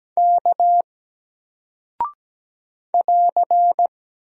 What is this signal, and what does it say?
Morse code, heard as a keyed tone, about 17 wpm, 700 Hz. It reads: K <AR>
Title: Single Character (Letter:Number:Punctuation:Prosign) Alt 17wpm